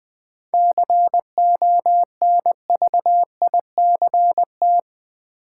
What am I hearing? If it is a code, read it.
Morse code, heard as a keyed tone, about 20 wpm, 700 Hz. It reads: CONVICT
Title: 7 Letter Words - First 3 Letters Given - 20wpm